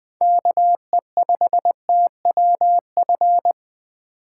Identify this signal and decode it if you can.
Morse code, heard as a keyed tone, about 21 wpm, 700 Hz. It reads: KE5TWF
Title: USA Call Signs - 6 Characters - 4x Character Spacing - 20wpm